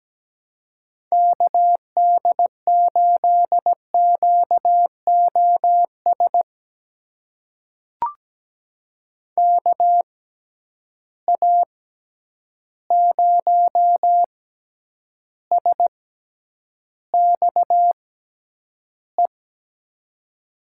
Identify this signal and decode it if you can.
Morse code, heard as a keyed tone, about 17 wpm, spaced as for 9 wpm, 700 Hz. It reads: KD8QOS KA0SXE
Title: USA Call Signs - 6 Characters - 6x Character Spacing - 17wpm